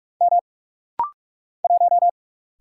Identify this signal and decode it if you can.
Morse code, heard as a keyed tone, about 44 wpm, 690 Hz. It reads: M 1
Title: Single Letter-Number - Rapid-Fire 45wpm